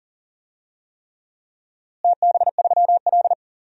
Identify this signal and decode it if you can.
Morse code, heard as a keyed tone, about 40 wpm, 700 Hz. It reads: TB3L
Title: DXCC-100 Call Signs - 4 Characters - Repeated 3x - 40wpm